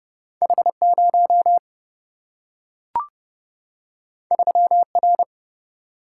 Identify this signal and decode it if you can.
Morse code, heard as a keyed tone, about 30 wpm, 700 Hz. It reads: H 0 3 R
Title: Single-Letter Single-Number 30wpm